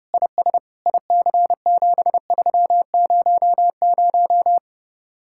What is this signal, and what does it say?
Morse code, heard as a keyed tone, about 30 wpm, 700 Hz. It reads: IS IC7300